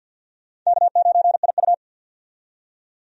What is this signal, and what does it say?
Morse code, heard as a keyed tone, about 47 wpm, 700 Hz. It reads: K9IU
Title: USA Call Signs - 4 Characters - 50wpm